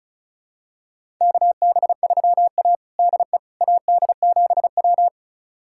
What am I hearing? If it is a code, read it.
Morse code, heard as a keyed tone, about 36 wpm, 700 Hz. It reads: KB3A DE AD7W
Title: USA Callsign DE Callsign - 4 Characters - 35wpm